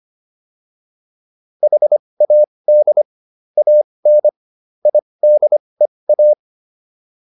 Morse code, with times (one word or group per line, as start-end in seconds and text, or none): 1.6-3.0 HAD
3.6-4.3 AN
4.9-6.3 IDEA